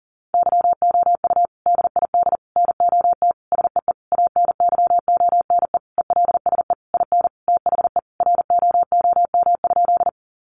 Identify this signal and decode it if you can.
Morse code, heard as a keyed tone, about 40 wpm, 700 Hz. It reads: YOU DID NOT SEE ANYONE ELSE IN THE ROOM?
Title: Sentences from Top 900 Words 40wpm